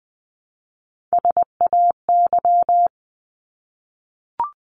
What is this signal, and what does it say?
Morse code, heard as a keyed tone, about 20 wpm, 700 Hz. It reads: SAY
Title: Top 200 Words 20wpm